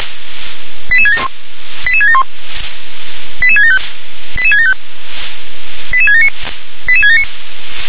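Five-tone selective call (ZVEI), heard as quick runs of five tones, tones of about 70 ms. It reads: 88671 88671 88675 88675 88679 88679
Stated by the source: ZVEI five-tone call received by SDR.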